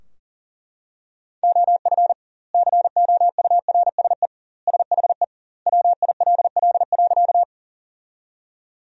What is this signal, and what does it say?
Morse code, heard as a keyed tone, about 37 wpm, 700 Hz. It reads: OF COURSE SHE WILL.
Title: 2x Word-Spacing - Sentences from Top 500 Words - 40wpm